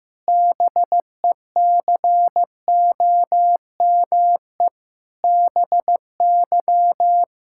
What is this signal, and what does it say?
Morse code, heard as a keyed tone, about 15 wpm, 700 Hz. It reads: BECOME BY
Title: Pangram Challenge v2 - 15wpm